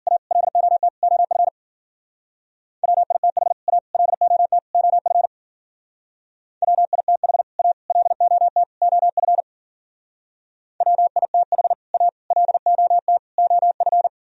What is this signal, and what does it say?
Morse code, heard as a keyed tone, about 50 wpm, 700 Hz. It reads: A LOT OF WITH A LOT OF WITH A LOT OF WITH A LOT OF
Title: Speed-Racing - Sets of 4 Words 40wpm